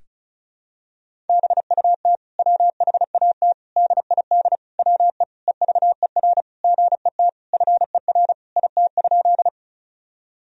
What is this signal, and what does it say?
Morse code, heard as a keyed tone, about 36 wpm, 700 Hz. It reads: BUT WHAT DID WE EVER GET FER IT?
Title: Sentences from Top 400 Words and Abbreviations 35wpm